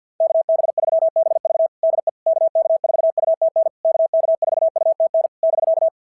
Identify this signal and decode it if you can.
Morse code, heard as a keyed tone, about 51 wpm, 690 Hz. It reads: KB3BV DE KK4UTN KK4UTN <BK>
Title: SOTA QSO Practice 50wpm